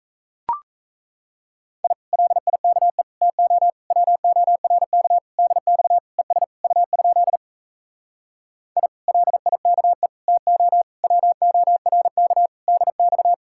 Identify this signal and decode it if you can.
Morse code, heard as a keyed tone, about 37 wpm, 700 Hz. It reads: I LIKE TO WORK DX ES U? I LIKE TO WORK DX